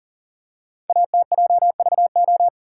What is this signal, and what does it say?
Morse code, heard as a keyed tone, about 39 wpm, 690 Hz. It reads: ATJVO
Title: Code Groups - 5 Letters - 40wpm